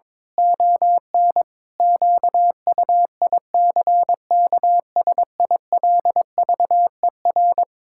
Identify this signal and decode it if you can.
Morse code, heard as a keyed tone, about 22 wpm, 700 Hz. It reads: ON QUICKSILVER
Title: Pangram Challenge v2 - 22wpm